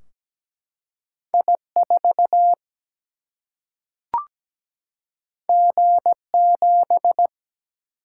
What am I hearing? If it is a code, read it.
Morse code, heard as a keyed tone, about 17 wpm, 700 Hz. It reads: I4 G7
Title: Single-Letter Single-Number 17wpm